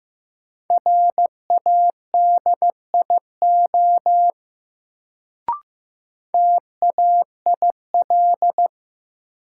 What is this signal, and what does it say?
Morse code, heard as a keyed tone, about 15 wpm, 700 Hz. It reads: RADIO TAIL
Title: Sentences from Top 1000 Words - Review Part 3 - New Word Drill - 15wpm